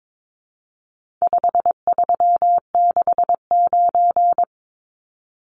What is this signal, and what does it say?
Morse code, heard as a keyed tone, about 22 wpm, 700 Hz. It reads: 5369
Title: Numbers - 4-Digit 22wpm